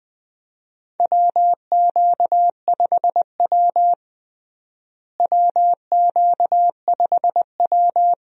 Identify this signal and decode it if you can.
Morse code, heard as a keyed tone, about 20 wpm, 700 Hz. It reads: WQ5W WQ5W
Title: USA Call Signs - 4 Characters - Repeated 2x - 20wpm